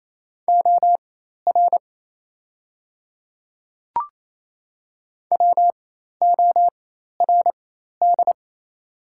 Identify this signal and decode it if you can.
Morse code, heard as a keyed tone, about 28 wpm, spaced as for 13 wpm, 700 Hz. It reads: OR WORD
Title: 4x Character Spacing - Top 300 Words - 28wpm